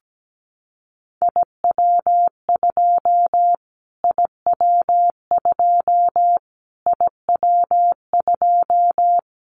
Morse code, heard as a keyed tone, about 17 wpm, 700 Hz. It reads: IW2 IW2 IW2
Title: Callsign Prefixes for Top 25 DXCC - Repeated 3x - 17wpm